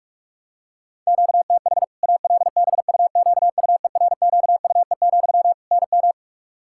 Morse code, heard as a keyed tone, about 46 wpm, 700 Hz. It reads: QTH ALBUQUERQUE, NM
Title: Practice QSO - Exchange 1 - Beginning - 45wpm